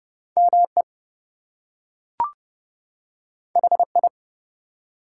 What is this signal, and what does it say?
Morse code, heard as a keyed tone, about 31 wpm, 700 Hz. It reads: ME HI
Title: Two Letter Words - 30wpm